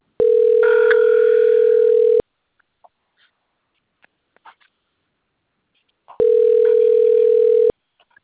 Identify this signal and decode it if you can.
Ringback tone (North America), 440 + 480 Hz.